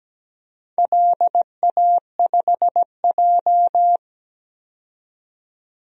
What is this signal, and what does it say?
Morse code, heard as a keyed tone, about 17 wpm, 700 Hz. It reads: LA5J
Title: DXCC-100 Call Signs - 4 Characters - 17wpm